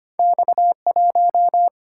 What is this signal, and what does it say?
Morse code, heard as a keyed tone, about 25 wpm, 700 Hz. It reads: X1
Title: Single-Letter Single-Number 25wpm